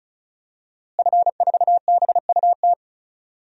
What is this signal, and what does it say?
Morse code, heard as a keyed tone, about 36 wpm, 700 Hz. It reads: F4BUT
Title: DXCC-100 Call Signs - 5 Characters - Repeated 3x - 35wpm